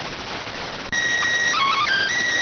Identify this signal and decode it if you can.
SSTV (slow-scan television), PD 290.